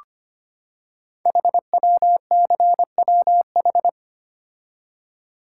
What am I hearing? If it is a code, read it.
Morse code, heard as a keyed tone, about 25 wpm, 700 Hz. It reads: HWCWH
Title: Code Groups - 5 Letters - 25wpm